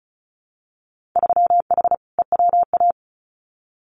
Morse code, heard as a keyed tone, about 36 wpm, 700 Hz. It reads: 3H EWA
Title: Winter Field Day Exchanges Encore 35wpm